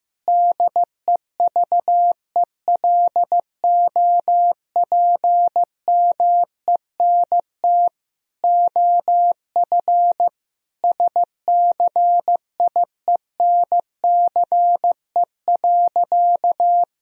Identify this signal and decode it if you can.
Morse code, heard as a keyed tone, about 15 wpm, 700 Hz. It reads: DEVELOPMENT OF SCIENCE.
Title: Sentences from Top 1000 Words - Review Part 4 - 15wpm